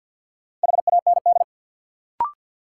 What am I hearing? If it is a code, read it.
Morse code, heard as a keyed tone, about 52 wpm, 700 Hz. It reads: HAND
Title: Sentences from Top 200 Words - New Word Drill - 50wpm